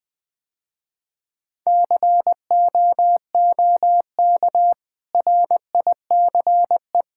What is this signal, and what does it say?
Morse code, heard as a keyed tone, about 20 wpm, 700 Hz. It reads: COOK RICE
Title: Easy Sentences with Hints - 20wpm